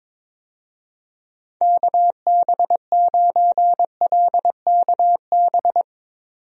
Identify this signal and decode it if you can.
Morse code, heard as a keyed tone, about 22 wpm, 700 Hz. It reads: KB9LKB
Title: USA Call Signs - 6 Characters - 22wpm